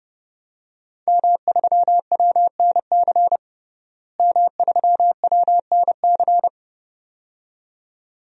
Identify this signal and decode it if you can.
Morse code, heard as a keyed tone, about 30 wpm, 700 Hz. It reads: M3WNC M3WNC
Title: DXCC-100 Call Signs - 5 Characters - Repeated 2x - 30wpm